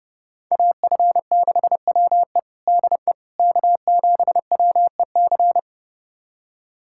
Morse code, heard as a keyed tone, about 29 wpm, 700 Hz. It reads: AF6WE DE K7WEC
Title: USA Callsign DE Callsign - 5 Characters - 30wpm